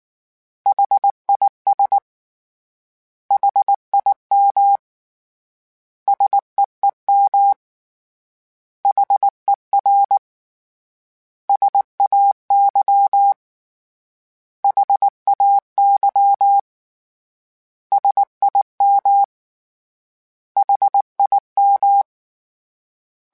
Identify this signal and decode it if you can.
Morse code, heard as a keyed tone, about 19 wpm, 800 Hz. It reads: HIS HIM SEEM HER SAY HAY SIM HIM
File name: S vs H / S vs H mp3s / S vs H @18